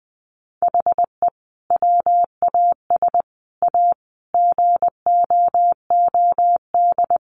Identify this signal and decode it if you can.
Morse code, heard as a keyed tone, about 20 wpm, 700 Hz. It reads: HE WAS A GOOD
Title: Sentences from Top 1300 Words 20wpm